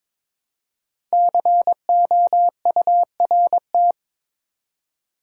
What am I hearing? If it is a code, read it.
Morse code, heard as a keyed tone, about 22 wpm, 700 Hz. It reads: COURT
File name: 5 Letter Words - First 3 Letters Given - 22wpm